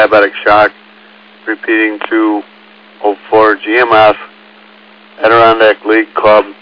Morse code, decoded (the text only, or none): none